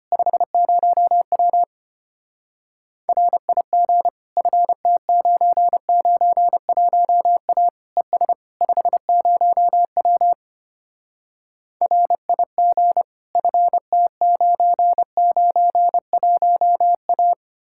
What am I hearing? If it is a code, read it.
Morse code, heard as a keyed tone, about 32 wpm, 700 Hz. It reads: 50W RIG FT991A ES 50W RIG FT991A